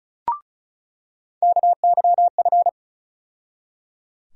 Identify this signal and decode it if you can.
Morse code, heard as a keyed tone, about 36 wpm, 700 Hz. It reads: KYF
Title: USA Call Sign Postfixes - 3 Letters 35wpm